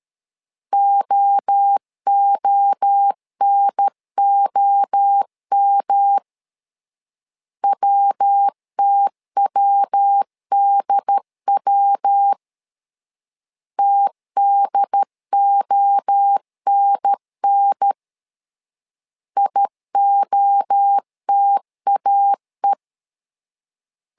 Morse code, heard as a keyed tone, about 13 wpm, 800 Hz. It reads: OONOM WTWDW TDONN IOTAE
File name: LEZIONI / 3 MWRO / 3 MWRO 12wpm 1730